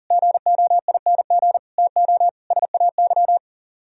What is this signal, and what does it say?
Morse code, heard as a keyed tone, about 56 wpm, spaced as for 48 wpm, 700 Hz. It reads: GOING TO SAY
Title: Speed-Racing - Sets of 5 Words 40wpm